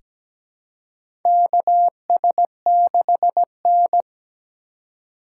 Morse code, heard as a keyed tone, about 17 wpm, 700 Hz. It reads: KS6N
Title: USA Call Signs - 4 Characters - 8x Character Spacing - 17wpm